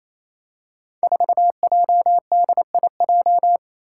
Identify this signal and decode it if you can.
Morse code, heard as a keyed tone, about 29 wpm, 700 Hz. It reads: 4JDIJ
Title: Five Challenging Letter-Number Combinations 28wpm